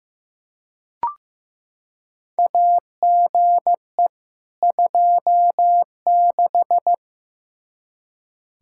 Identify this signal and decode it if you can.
Morse code, heard as a keyed tone, about 15 wpm, 700 Hz. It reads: AGE 26
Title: Practice QSO - Exchange 3 - More Info - 15wpm